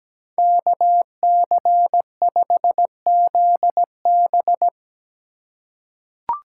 Morse code, heard as a keyed tone, about 17 wpm, 700 Hz. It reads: KC5ZB
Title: USA Call Signs - 4x Character Spacing - 17wpm